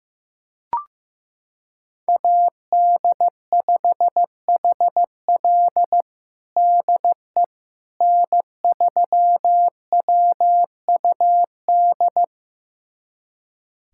Morse code, heard as a keyed tone, about 15 wpm, 700 Hz. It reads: AD5HL DE N3WUD
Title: USA Callsign DE Callsign - 5 Characters - 15wpm